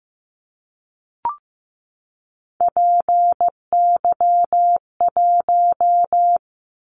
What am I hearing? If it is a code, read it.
Morse code, heard as a keyed tone, about 15 wpm, 700 Hz. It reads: PY1
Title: Callsign Prefixes for Top 100 DXCC - 15wpm